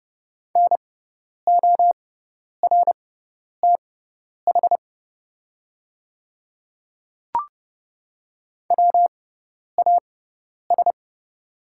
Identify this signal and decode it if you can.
Morse code, heard as a keyed tone, about 30 wpm, spaced as for 10 wpm, 690 Hz. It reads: NORTH WAS
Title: 6x Character Spacing - Top 300 Words - 30wpm